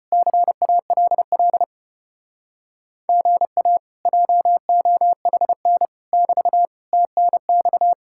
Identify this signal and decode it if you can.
Morse code, heard as a keyed tone, about 33 wpm, 700 Hz. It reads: CALL GA JOHN <BT> TNX